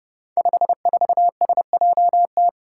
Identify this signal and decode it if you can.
Morse code, heard as a keyed tone, about 30 wpm, 700 Hz. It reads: 54SJT
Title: Five Challenging Letter-Number Combinations 30wpm